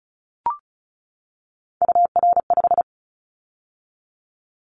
Morse code, heard as a keyed tone, about 36 wpm, 700 Hz. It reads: UR5
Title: Callsign Prefixes for Top 25 DXCC - 35wpm